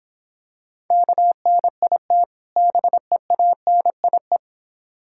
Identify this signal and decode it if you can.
Morse code, heard as a keyed tone, about 25 wpm, 700 Hz. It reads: KNIT BEANIE
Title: Difficult Words - Speed-Racing with Hints - 20wpm